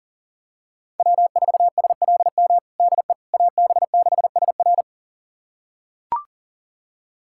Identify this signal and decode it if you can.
Morse code, heard as a keyed tone, about 40 wpm, 700 Hz. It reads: W4SLM DE AB6SR
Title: USA Callsign DE Callsign - 5 Characters - 40wpm